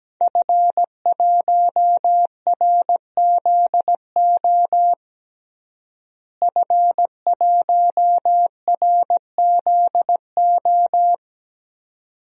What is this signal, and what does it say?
Morse code, heard as a keyed tone, about 17 wpm, 700 Hz. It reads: F1RZO F1RZO
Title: DXCC-100 Call Signs - 5 Characters - Repeated 2x - 17wpm